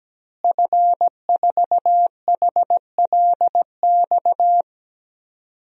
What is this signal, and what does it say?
Morse code, heard as a keyed tone, about 17 wpm, 700 Hz. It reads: F4HLX
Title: DXCC-100 Call Signs - 5 Characters - Repeated 2x - 17wpm